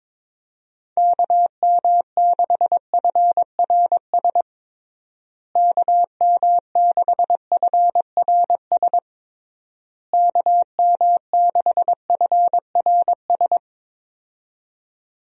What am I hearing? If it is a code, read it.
Morse code, heard as a keyed tone, about 22 wpm, 700 Hz. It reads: KM6FRS KM6FRS KM6FRS
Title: USA Call Signs - 6 Characters - Repeated 3x - 22wpm